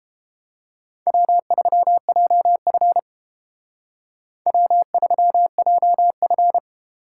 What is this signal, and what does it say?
Morse code, heard as a keyed tone, about 30 wpm, 700 Hz. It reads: W3JF W3JF